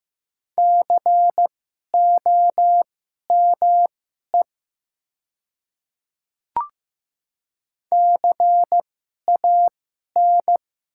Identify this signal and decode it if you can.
Morse code, heard as a keyed tone, about 15 wpm, spaced as for 11 wpm, 700 Hz. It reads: COME CAN
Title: 2x Character Spacing - Top 100 Words - 15wpm